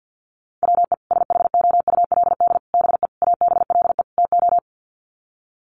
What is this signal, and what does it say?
Morse code, heard as a keyed tone, about 50 wpm, 700 Hz. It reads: WE SHOULD BE ABLE TO
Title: Sets of 5 Words Encore2 50wpm